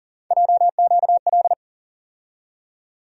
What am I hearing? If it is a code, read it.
Morse code, heard as a keyed tone, about 40 wpm, 700 Hz. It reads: JQL
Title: USA Call Sign Postfixes - 3 Letters 40wpm